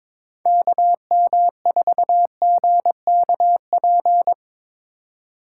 Morse code, heard as a keyed tone, about 25 wpm, 690 Hz. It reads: KM4GKP
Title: Speed Racing - USA Call Signs 22wpm